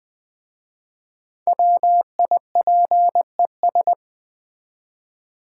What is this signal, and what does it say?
Morse code, heard as a keyed tone, about 20 wpm, 700 Hz. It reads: WIPES
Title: Easy Words with Hints - 20wpm